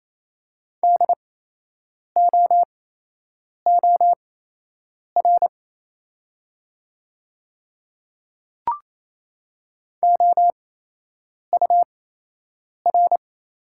Morse code, heard as a keyed tone, about 28 wpm, spaced as for 8 wpm, 700 Hz. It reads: DOOR OUR
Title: 8x Character Spacing - Top 500 Words - 28wpm